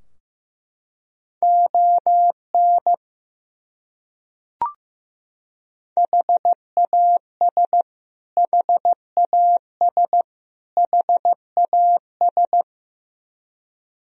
Morse code, heard as a keyed tone, about 15 wpm, 700 Hz. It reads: ON HAS HAS HAS
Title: Top 100 Words Repeated 15wpm